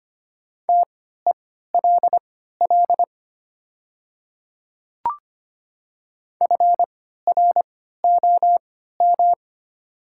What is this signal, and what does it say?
Morse code, heard as a keyed tone, about 25 wpm, spaced as for 14 wpm, 690 Hz. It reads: TELL FROM
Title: 3x Character Spacing - Top 200 Words - 25wpm